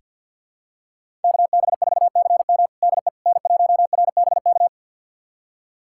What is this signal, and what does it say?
Morse code, heard as a keyed tone, about 46 wpm, 700 Hz. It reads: KB4CM DE N1RBK
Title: USA Callsign DE Callsign - 5 Characters - 50wpm